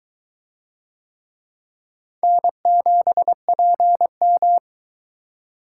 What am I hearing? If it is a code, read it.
Morse code, heard as a keyed tone, about 21 wpm, 700 Hz. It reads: N7PM